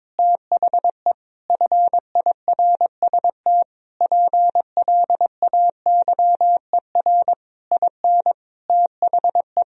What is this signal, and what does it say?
Morse code, heard as a keyed tone, about 22 wpm, 700 Hz. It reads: THE FIRST PLAYER IN THE
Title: Sentences from Top 1000 Words - Review Part 4 - 22wpm